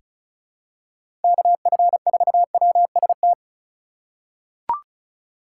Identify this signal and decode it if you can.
Morse code, heard as a keyed tone, about 36 wpm, 700 Hz. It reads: KF4WST
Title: USA Call Signs - 3x Character Spacing - 35wpm